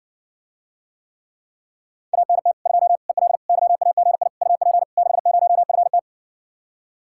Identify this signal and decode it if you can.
Morse code, heard as a keyed tone, about 61 wpm, 700 Hz. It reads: ANT 2 EL YAGI UP 60FT